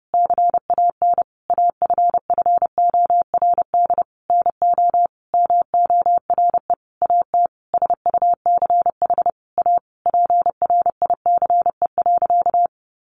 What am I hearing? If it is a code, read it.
Morse code, heard as a keyed tone, about 28 wpm, 700 Hz. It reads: CAN AFFORD NO MORE AT SUCH A PRICE.
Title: Sentences from Top 1100 Words 30wpm